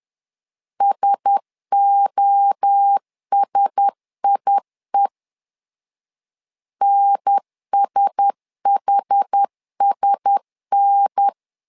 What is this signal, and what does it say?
Morse code, heard as a keyed tone, about 10 wpm, 800 Hz. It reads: SOSIE NSHSN
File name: LEZIONI / 4 Q1S 10wpm 2133